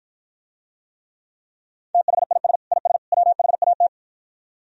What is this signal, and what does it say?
Morse code, heard as a keyed tone, about 52 wpm, 700 Hz. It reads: THIS IS WHAT